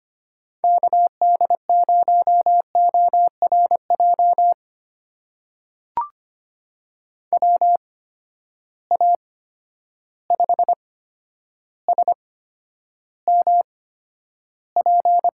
Morse code, heard as a keyed tone, about 25 wpm, spaced as for 11 wpm, 700 Hz. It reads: KD0ORJ WA5SMP
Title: USA Call Signs - 6 Characters - 8x Character Spacing - 25wpm